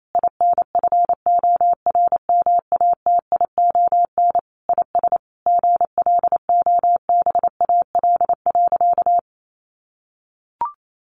Morse code, heard as a keyed tone, about 28 wpm, 700 Hz. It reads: INFORMATION IS GLOBAL.
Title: Sentences from Top 1200 Words 28wpm